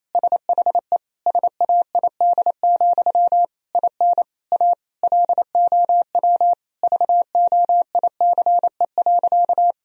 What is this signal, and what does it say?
Morse code, heard as a keyed tone, about 28 wpm, 700 Hz. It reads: SHE SAID, IN A LOW VOICE.
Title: Sentences from Top 1000 Words - Review Part 1 - 28wpm